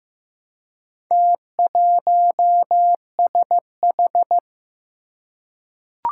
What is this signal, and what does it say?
Morse code, heard as a keyed tone, about 15 wpm, 700 Hz. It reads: T1SH
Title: Four Challenging Letter-Number Combinations 15wpm